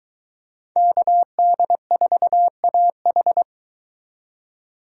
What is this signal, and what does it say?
Morse code, heard as a keyed tone, about 22 wpm, 700 Hz. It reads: KD4AH